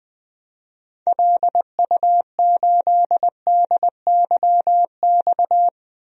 Morse code, heard as a keyed tone, about 20 wpm, 700 Hz. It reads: LU8DYX
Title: DXCC-100 Call Signs - 6 Characters - Repeated 3x - 20wpm